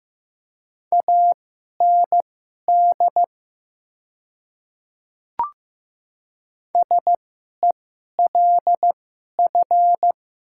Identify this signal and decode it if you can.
Morse code, heard as a keyed tone, about 15 wpm, spaced as for 11 wpm, 700 Hz. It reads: AND SELF
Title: 2x Character Spacing - Top 300 Words - 15wpm